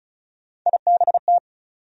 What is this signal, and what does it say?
Morse code, heard as a keyed tone, about 34 wpm, 700 Hz. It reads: IBT
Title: USA Call Sign Postfixes - 3 Letters 35wpm